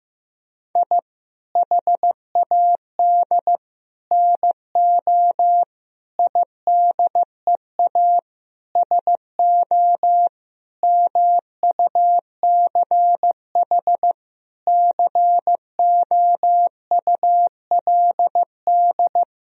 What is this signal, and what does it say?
Morse code, heard as a keyed tone, about 15 wpm, 700 Hz. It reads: I HAD NO IDEA SO MUCH COULD
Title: Sentences from Top 1000 Words - Review Part 4 - 15wpm